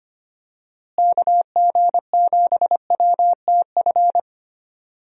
Morse code, heard as a keyed tone, about 25 wpm, 700 Hz. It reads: KG7WTF
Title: USA Call Signs - 6 Characters - Repeated 2x - 25wpm